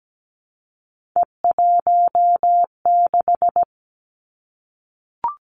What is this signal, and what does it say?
Morse code, heard as a keyed tone, about 17 wpm, 700 Hz. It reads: E16
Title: Three Challenging Letter-Number Combinations 17wpm